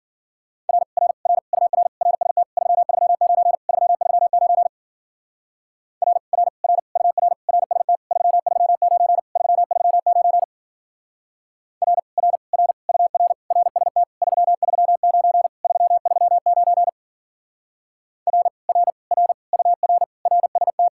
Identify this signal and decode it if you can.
Morse code, heard as a keyed tone, about 49 wpm, 700 Hz. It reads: R R R UR RST 339 339 R R R UR RST 339 339 R R R UR RST 339 339 R R R UR RST
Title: Speed-Racing Common QSO Phrases - 40wpm